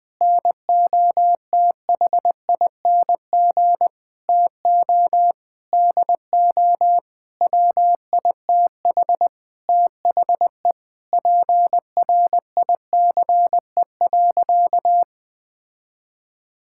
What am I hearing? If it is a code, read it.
Morse code, heard as a keyed tone, about 20 wpm, 700 Hz. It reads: NOTHING TO DO WITH THE PRICE.
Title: Sentences with 7-Word Phrases - 20wpm